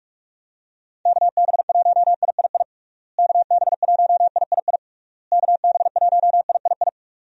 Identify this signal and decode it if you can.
Morse code, heard as a keyed tone, about 46 wpm, 700 Hz. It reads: KB1III KB1III KB1III
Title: USA Call Signs - 6 Characters - Repeated 3x - 45wpm